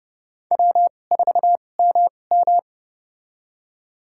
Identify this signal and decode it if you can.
Morse code, heard as a keyed tone, about 30 wpm, spaced as for 22 wpm, 700 Hz. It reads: W4MM
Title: USA Call Signs - 4 Characters - 2x Character Spacing - 30wpm